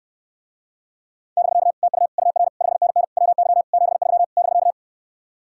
Morse code, heard as a keyed tone, about 67 wpm, 700 Hz. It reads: <BK> TU UR 5NN WY 73 <BK>